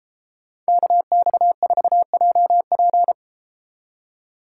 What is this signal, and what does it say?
Morse code, heard as a keyed tone, about 33 wpm, 700 Hz. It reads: KX4JP